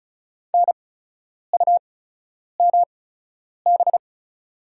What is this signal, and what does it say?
Morse code, heard as a keyed tone, about 34 wpm, spaced as for 10 wpm, 690 Hz. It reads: NUMB
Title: 8x Character Spacing - Top 500 Words - 35wpm